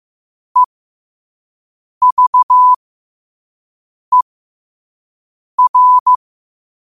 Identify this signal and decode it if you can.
Morse code, heard as a keyed tone, about 15 wpm, spaced as for 5 wpm, 1000 Hz. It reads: EVER
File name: qotdfiles / 5wpm